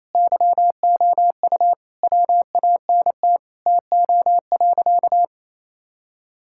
Morse code, heard as a keyed tone, about 27 wpm, 700 Hz. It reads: YOU WANT TO.
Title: Sentences from Top 1600 Words 28wpm